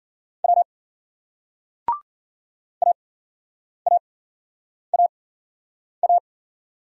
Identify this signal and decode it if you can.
Morse code, heard as a keyed tone, about 53 wpm, spaced as for 9 wpm, 700 Hz. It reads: W AAAA